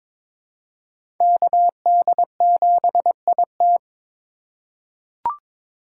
Morse code, heard as a keyed tone, about 22 wpm, 700 Hz. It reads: KD7IT
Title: USA Call Signs - 2x Character Spacing - 22wpm